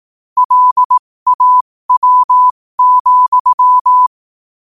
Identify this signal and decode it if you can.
Morse code, heard as a keyed tone, about 18 wpm, spaced as for 15 wpm, 1000 Hz. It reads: LAW,
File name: qotdfiles / 15wpm